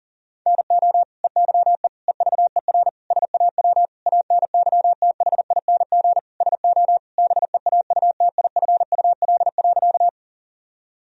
Morse code, heard as a keyed tone, about 37 wpm, 700 Hz. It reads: NO EYE EVER SAW ANYTHING SO BEAUTIFUL.
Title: Sentences from Top 1000 Words - Review Part 1 - 40wpm